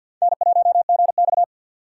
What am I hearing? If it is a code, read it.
Morse code, heard as a keyed tone, about 52 wpm, 700 Hz. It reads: N1GX
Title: USA Call Signs - 4 Characters - 2x Character Spacing - 50wpm